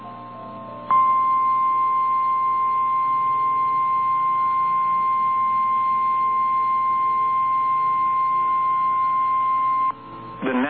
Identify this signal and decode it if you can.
Emergency Alert System (SAME) tones (the 1050 Hz attention tone), with speech. It is heard from 0.8 to 10.0 s.
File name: eas / KNCYKBIE / KNCYKBIE